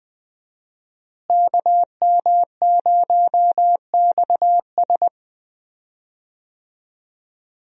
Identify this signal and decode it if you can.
Morse code, heard as a keyed tone, about 20 wpm, 700 Hz. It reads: KM0XS
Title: USA Call Signs - 5 Characters - Repeated 2x - 20wpm